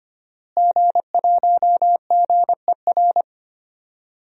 Morse code, heard as a keyed tone, about 25 wpm, 700 Hz. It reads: G1GER
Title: DXCC-100 Call Signs - 5 Characters - Repeated 3x - 25wpm